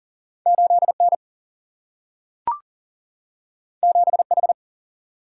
Morse code, heard as a keyed tone, about 39 wpm, 700 Hz. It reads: 8N 7H
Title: Single-Letter Single-Number 40wpm